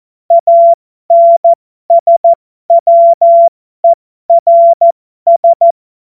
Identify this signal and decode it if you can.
Morse code, heard as a keyed tone, about 14 wpm, spaced as for 12 wpm, 680 Hz. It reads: ANSWERS
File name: Morsekurs / Witze